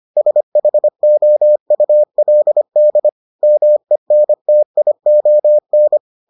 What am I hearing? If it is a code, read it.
Morse code, heard as a keyed tone, about 25 wpm, 600 Hz. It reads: SHOULD MENTION